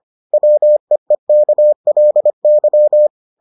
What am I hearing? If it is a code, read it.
Morse code, heard as a keyed tone, about 25 wpm, 600 Hz. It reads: WEEKLY